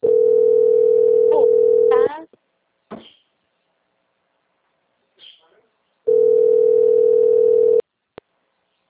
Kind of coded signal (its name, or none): telephone call progress tone